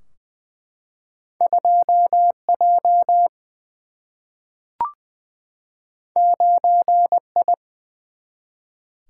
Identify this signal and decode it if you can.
Morse code, heard as a keyed tone, about 20 wpm, 700 Hz. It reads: 2 J 9 I
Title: Single-Letter Single-Number 20wpm